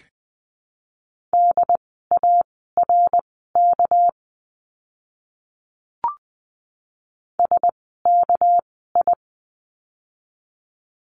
Morse code, heard as a keyed tone, about 20 wpm, spaced as for 14 wpm, 700 Hz. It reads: DARK SKI
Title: 199 Lesson 32f - Letter k - Words 20wpm